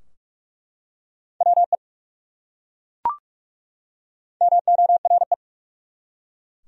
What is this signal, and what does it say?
Morse code, heard as a keyed tone, about 43 wpm, 700 Hz. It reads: WE MORE
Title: Top 100 Words 45wpm